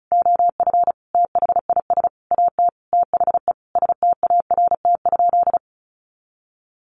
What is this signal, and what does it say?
Morse code, heard as a keyed tone, about 34 wpm, 700 Hz. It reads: OF THIS AT THE START?
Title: Sentences from Top 1500 Words 35wpm